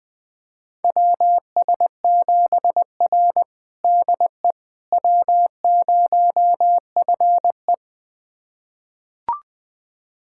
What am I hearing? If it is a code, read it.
Morse code, heard as a keyed tone, about 21 wpm, 700 Hz. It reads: WS7R DE W0FE
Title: USA Callsign DE Callsign - 4 Characters - 20wpm